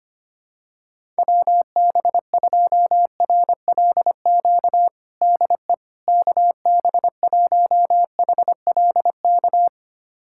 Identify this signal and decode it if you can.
Morse code, heard as a keyed tone, about 25 wpm, 700 Hz. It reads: WB2RLQ DE KB1HLK
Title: USA Callsign DE Callsign - 6 Characters - 25wpm